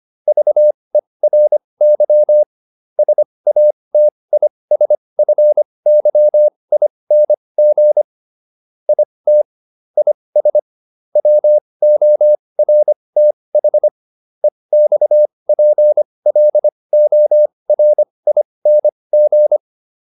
Morse code, heard as a keyed tone, about 25 wpm, spaced as for 20 wpm, 600 Hz. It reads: VERY SATISFYING IT IS WORTH EXPLORING
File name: SS206 20